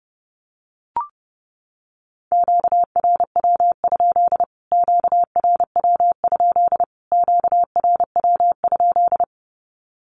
Morse code, heard as a keyed tone, about 30 wpm, 700 Hz. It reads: QRW? QRW? QRW?
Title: Q-Codes Repeated 3x 30wpm